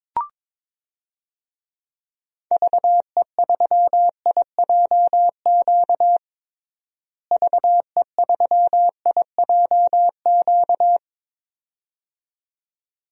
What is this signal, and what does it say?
Morse code, heard as a keyed tone, about 22 wpm, 700 Hz. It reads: VE3IJQ VE3IJQ
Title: DXCC-100 Call Signs - 6 Characters - Repeated 2x - 22wpm